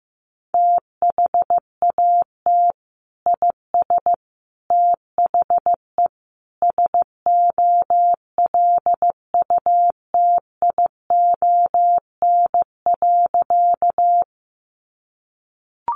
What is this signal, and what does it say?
Morse code, heard as a keyed tone, about 15 wpm, 700 Hz. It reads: THAT IS THE SOLUTION.
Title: Sentences from Top 1300 Words 15wpm